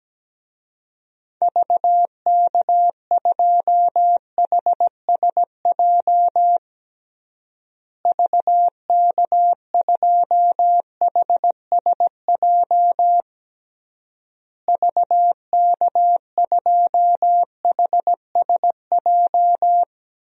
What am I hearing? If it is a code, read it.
Morse code, heard as a keyed tone, about 17 wpm, 700 Hz. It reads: VK2HSJ VK2HSJ VK2HSJ
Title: DXCC-100 Call Signs - 6 Characters - Repeated 3x - 17wpm